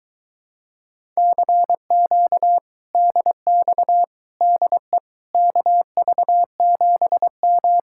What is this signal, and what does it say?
Morse code, heard as a keyed tone, about 22 wpm, 700 Hz. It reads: CQ DX DE KV7M